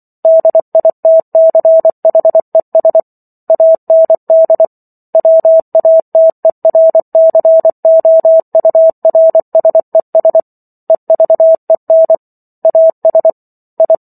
Morse code, heard as a keyed tone, about 24 wpm, 650 Hz. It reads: DITCHES AND WATERCOURSES EVEN AS I